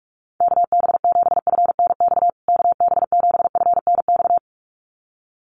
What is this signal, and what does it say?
Morse code, heard as a keyed tone, about 44 wpm, 690 Hz. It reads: KB7FNX KB7FNX
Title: USA Call Signs Repeated 2x 45wpm